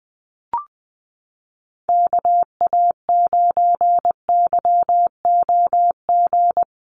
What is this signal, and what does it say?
Morse code, heard as a keyed tone, about 20 wpm, 700 Hz. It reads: KA9YOG
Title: USA Call Signs Repeated 2x 20wpm